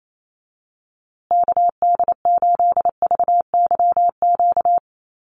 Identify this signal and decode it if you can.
Morse code, heard as a keyed tone, about 28 wpm, 700 Hz. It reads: KD8VYQ
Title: USA Call Signs Repeated 2x 28wpm